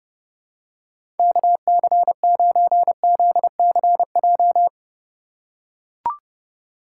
Morse code, heard as a keyed tone, about 31 wpm, 700 Hz. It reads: KC9ZCJ
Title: USA Call Signs - 3x Character Spacing - 30wpm